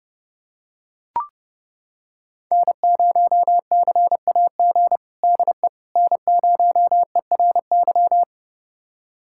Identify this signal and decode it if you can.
Morse code, heard as a keyed tone, about 30 wpm, 700 Hz. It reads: N0CAG DE N0ERY
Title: USA Callsign DE Callsign - 5 Characters - 30wpm